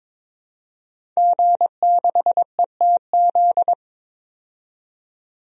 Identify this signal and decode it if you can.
Morse code, heard as a keyed tone, about 22 wpm, 700 Hz. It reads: G6ETZ
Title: DXCC-100 Call Signs - 5 Characters - 22wpm